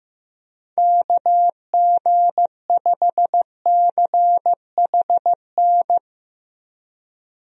Morse code, heard as a keyed tone, about 15 wpm, 700 Hz. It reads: KG5CHN
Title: USA Call Signs - 6 Characters - Repeated 3x - 15wpm